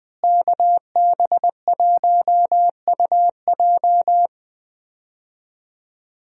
Morse code, heard as a keyed tone, about 20 wpm, 700 Hz. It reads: KB1UJ
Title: USA Call Signs - 5 Characters - Repeated 3x - 20wpm